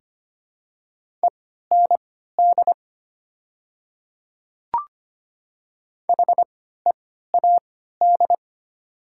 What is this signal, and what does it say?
Morse code, heard as a keyed tone, about 25 wpm, spaced as for 14 wpm, 700 Hz. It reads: END HEAD
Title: 3x Character Spacing - Top 500 Words - 25wpm